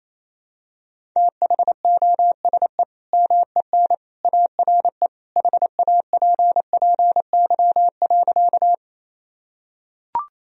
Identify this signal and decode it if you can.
Morse code, heard as a keyed tone, about 28 wpm, 700 Hz. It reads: THOSE MEN ARE HAPPY.
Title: Sentences from Top 1000 Words - Review Part 1 - 28wpm